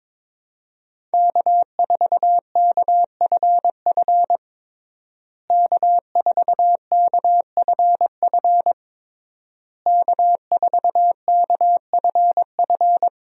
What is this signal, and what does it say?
Morse code, heard as a keyed tone, about 22 wpm, 700 Hz. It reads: K4KFF K4KFF K4KFF
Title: USA Call Signs - 5 Characters - Repeated 3x - 22wpm